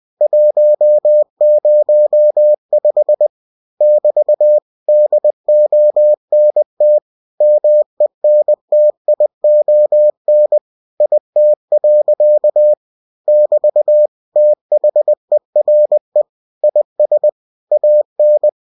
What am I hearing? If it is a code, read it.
Morse code, heard as a keyed tone, about 20 wpm, 600 Hz. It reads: 105 <BT> DONT MENTION IT. <BT> THERE IS AN
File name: cw-books / winnie-the-pooh / winnie-20wpm-page-0105